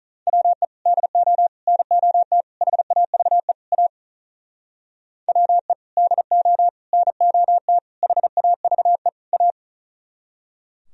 Morse code, heard as a keyed tone, about 45 wpm, 700 Hz. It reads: WE DO NOT HAVE A WE DO NOT HAVE A